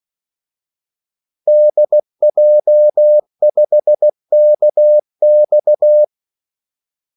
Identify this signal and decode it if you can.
Morse code, heard as a keyed tone, about 16 wpm, 600 Hz. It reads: DJ5KX
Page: Kaldesignaler 16 wpm